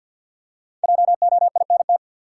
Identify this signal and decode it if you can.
Morse code, heard as a keyed tone, about 48 wpm, 700 Hz. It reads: JOINT
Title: Five Letter Words - 50wpm